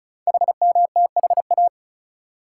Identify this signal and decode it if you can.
Morse code, heard as a keyed tone, about 36 wpm, 700 Hz. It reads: HMTHA
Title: Code Groups - 5 Letters - 35wpm